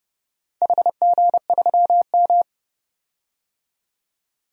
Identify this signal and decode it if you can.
Morse code, heard as a keyed tone, about 30 wpm, 700 Hz. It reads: HG3M
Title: DXCC-100 Call Signs - 4 Characters - 30wpm